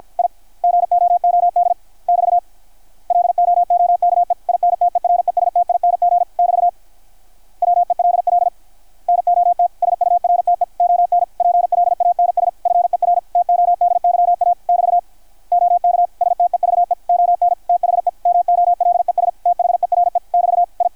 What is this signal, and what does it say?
Morse code, heard as a keyed tone, about 53 wpm, 690 Hz. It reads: I GOOD <BT> LOOKE INTERESTING <BT> WELL NOT SURTE ON PLANS FER TODYA <BT> OK STEVE ON THE MOLES THERE <BT> I
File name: archived / CRSnet